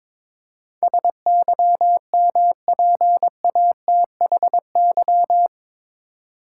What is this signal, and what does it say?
Morse code, heard as a keyed tone, about 21 wpm, 700 Hz. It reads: SYMPATHY
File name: practice / 22wpm / Eight Letter Words - 22wpm